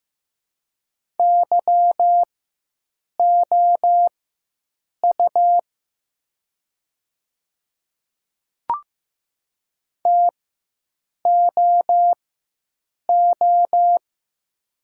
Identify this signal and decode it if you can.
Morse code, heard as a keyed tone, about 15 wpm, spaced as for 7 wpm, 700 Hz. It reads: YOU TOO
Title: 4x Character Spacing - Top 200 Words - 15wpm